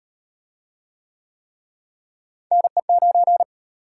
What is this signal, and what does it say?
Morse code, heard as a keyed tone, about 38 wpm, 700 Hz. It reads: NE9